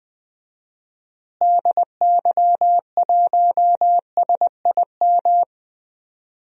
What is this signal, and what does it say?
Morse code, heard as a keyed tone, about 20 wpm, 700 Hz. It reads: DY1SIM
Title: DXCC-100 Call Signs - 6 Characters - Repeated 3x - 20wpm